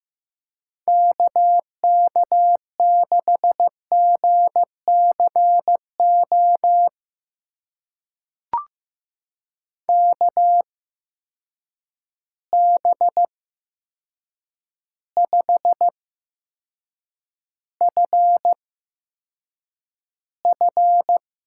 Morse code, heard as a keyed tone, about 15 wpm, spaced as for 7 wpm, 700 Hz. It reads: KK6GCO KB5FF
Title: USA Call Signs - 6 Characters - 8x Character Spacing - 15wpm